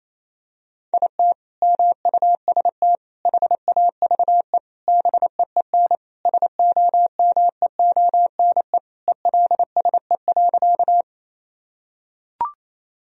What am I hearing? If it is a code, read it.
Morse code, heard as a keyed tone, about 28 wpm, 700 Hz. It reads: IT MUST HAVE BEEN SOMEONE ELSE.
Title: Sentences from Top 1000 Words - Review Part 4 - 28wpm